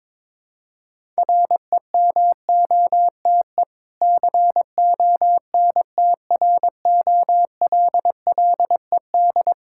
Morse code, heard as a keyed tone, about 22 wpm, 700 Hz. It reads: REMOTE CONTROLLED
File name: Difficult Words with Hints - 22wpm